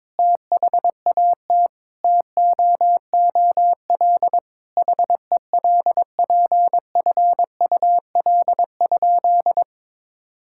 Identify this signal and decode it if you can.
Morse code, heard as a keyed tone, about 22 wpm, 700 Hz. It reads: THAT TOOL HELPFUL?
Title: Sentences from Top 1000 Words - Review Part 6 - 22wpm